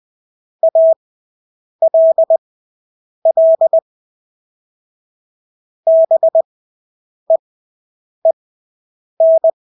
Morse code, heard as a keyed tone, about 20 wpm, spaced as for 8 wpm, 650 Hz. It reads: ALL BEEN
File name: Engelske ord 20/8 wpm